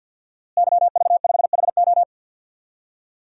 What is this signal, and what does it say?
Morse code, heard as a keyed tone, about 53 wpm, 700 Hz. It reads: YV5HO
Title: DXCC-100 Call Signs - 5 Characters - Repeated 2x - 50wpm